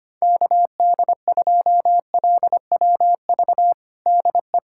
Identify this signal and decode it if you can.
Morse code, heard as a keyed tone, about 25 wpm, 700 Hz. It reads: KD2LWV DE
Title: USA Callsign DE Callsign - 6 Characters - 25wpm